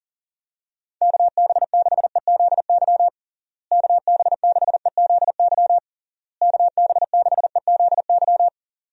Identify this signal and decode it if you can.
Morse code, heard as a keyed tone, about 37 wpm, 700 Hz. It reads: KB6EZY KB6EZY KB6EZY
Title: USA Call Signs - 6 Characters - Repeated 3x - 40wpm